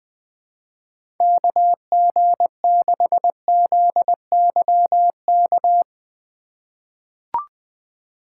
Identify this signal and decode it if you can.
Morse code, heard as a keyed tone, about 20 wpm, 700 Hz. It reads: KG6ZYK
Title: USA Call Signs - 6 Characters - 3x Character Spacing - 20wpm